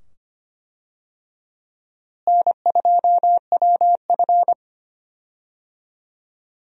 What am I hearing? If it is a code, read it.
Morse code, heard as a keyed tone, about 25 wpm, 700 Hz. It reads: N2WF
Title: USA Call Signs - 4 Characters - Repeated 2x - 25wpm